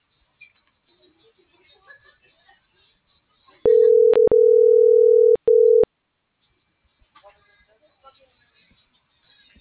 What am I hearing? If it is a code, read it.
Ringback tone (North America), 440 + 480 Hz.